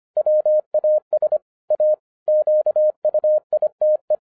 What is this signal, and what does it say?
Morse code, heard as a keyed tone, about 26 wpm, 610 Hz. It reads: WAS A QUITE
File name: 2011-10-mp3 / n04-JUST MOVED 25wpm